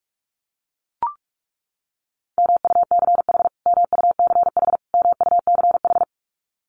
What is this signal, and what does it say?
Morse code, heard as a keyed tone, about 44 wpm, 700 Hz. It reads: MUCH MUCH MUCH
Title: Top 500 Words Repeated 45wpm